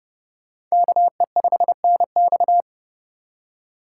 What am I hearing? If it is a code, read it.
Morse code, heard as a keyed tone, about 36 wpm, 700 Hz. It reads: KE5NX
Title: Speed Racing - USA 2x2 Call Signs 30wpm